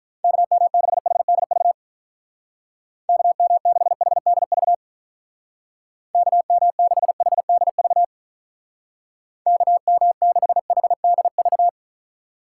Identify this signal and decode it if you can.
Morse code, heard as a keyed tone, about 47 wpm, 700 Hz. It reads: KM6HDV KM6HDV KM6HDV KM6HDV